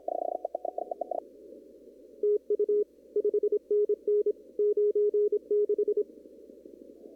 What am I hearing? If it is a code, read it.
Morse code, heard as a keyed tone, about 26 wpm, 420 Hz. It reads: TU 5NN 96
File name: CW / kt5j wpx cw-op-k5pi